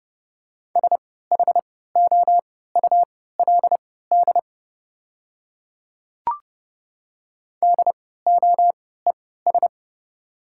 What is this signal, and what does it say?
Morse code, heard as a keyed tone, about 30 wpm, spaced as for 17 wpm, 700 Hz. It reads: SHOULD DOES
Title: 3x Character Spacing - Top 300 Words - 30wpm